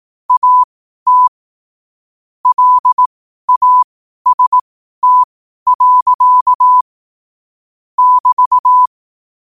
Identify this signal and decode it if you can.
Morse code, heard as a keyed tone, about 18 wpm, spaced as for 12 wpm, 1000 Hz. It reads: AT LAST. <BT>
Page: New quotes every day in morse code at 12 Words per minute.